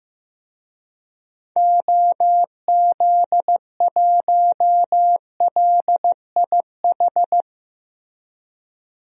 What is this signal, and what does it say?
Morse code, heard as a keyed tone, about 15 wpm, 700 Hz. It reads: OZ1LIH
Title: DXCC-100 Call Signs - 6 Characters - 15wpm